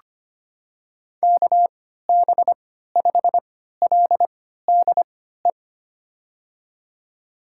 Morse code, heard as a keyed tone, about 25 wpm, spaced as for 14 wpm, 700 Hz. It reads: KB5LDE
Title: USA Call Signs - 6 Characters - 3x Character Spacing - 25wpm